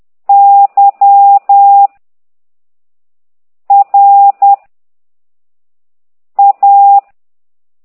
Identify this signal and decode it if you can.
Morse code, heard as a keyed tone, about 10 wpm, 800 Hz. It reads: Y R A